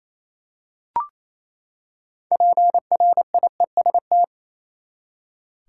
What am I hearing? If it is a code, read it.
Morse code, heard as a keyed tone, about 28 wpm, 700 Hz. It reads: PRIEST
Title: Six Letter Words - 28wpm